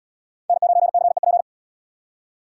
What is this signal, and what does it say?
Morse code, heard as a keyed tone, about 77 wpm, 700 Hz. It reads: N9CEP